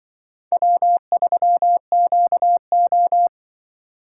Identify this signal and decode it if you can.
Morse code, heard as a keyed tone, about 25 wpm, 700 Hz. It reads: W3QO